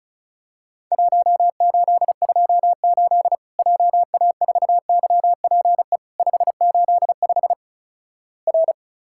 Morse code, heard as a keyed tone, about 36 wpm, 700 Hz. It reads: 1828 JA4YPE 585 R
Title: Worked All Europe DX Contest - QTC Exchange - 35wpm